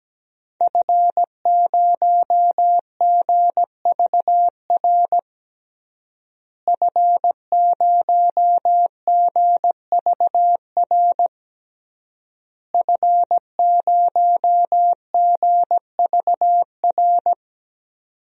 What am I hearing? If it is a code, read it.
Morse code, heard as a keyed tone, about 17 wpm, 700 Hz. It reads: F0GVR F0GVR F0GVR
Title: DXCC-100 Call Signs - 5 Characters - Repeated 3x - 17wpm